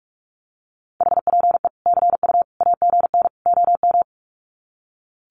Morse code, heard as a keyed tone, about 44 wpm, 700 Hz. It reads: HPE CU AGN OM
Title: Common QSO Phrases 45wpm